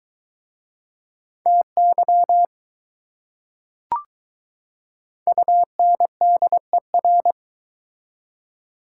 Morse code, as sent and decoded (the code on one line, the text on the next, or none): - -.-- / ..- -. -.. .-.
TY UNDER